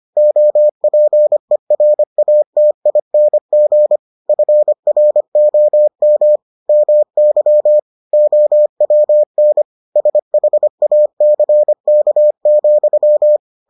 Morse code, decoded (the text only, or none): OPERATING FROM MY OWN SHACK,